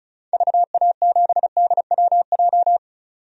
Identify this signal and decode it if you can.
Morse code, heard as a keyed tone, about 34 wpm, 700 Hz. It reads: VA7DWJ
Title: DXCC-100 Call Signs - 6 Characters - 35wpm